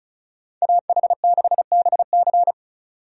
Morse code, heard as a keyed tone, about 34 wpm, 700 Hz. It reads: AH6BC
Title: USA Call Signs - 5 Characters - 35wpm